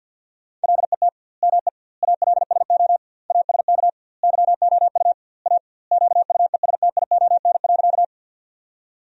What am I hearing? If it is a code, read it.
Morse code, heard as a keyed tone, about 52 wpm, 700 Hz. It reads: LET ME ALSO ASK YOU A QUESTION.
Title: 2x Word-Spacing - Sentences from Top 400 Words - 50wpm